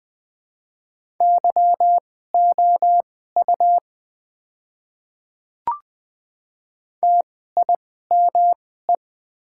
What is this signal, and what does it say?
Morse code, heard as a keyed tone, about 20 wpm, spaced as for 14 wpm, 690 Hz. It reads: YOU TIME
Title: 2x Character Spacing - Top 100 Words - 20wpm